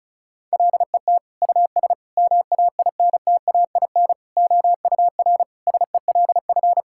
Morse code, heard as a keyed tone, about 36 wpm, 700 Hz. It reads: LET US MAINTAIN OUR SELF
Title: Sentences from Top 1000 Words - Review Part 5 - 35wpm